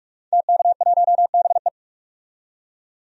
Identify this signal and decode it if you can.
Morse code, heard as a keyed tone, about 46 wpm, 700 Hz. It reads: TK1BE
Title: DXCC-100 Call Signs - 5 Characters - 45wpm